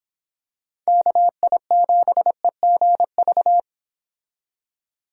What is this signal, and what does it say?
Morse code, heard as a keyed tone, about 25 wpm, 700 Hz. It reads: KI7EGV